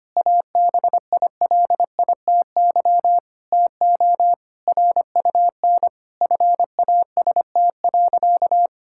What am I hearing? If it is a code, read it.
Morse code, heard as a keyed tone, about 25 wpm, 700 Hz. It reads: ABILITY TO RUN FAST.
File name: practice / 25wpm / Sentences with 3-Word Phrases - 25wpm